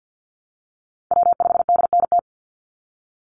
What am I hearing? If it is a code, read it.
Morse code, heard as a keyed tone, about 50 wpm, 700 Hz. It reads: W5DNT
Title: Call Signs Repeated 2x - 50wpm